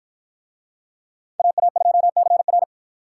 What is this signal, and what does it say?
Morse code, heard as a keyed tone, about 48 wpm, 700 Hz. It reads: AA2CR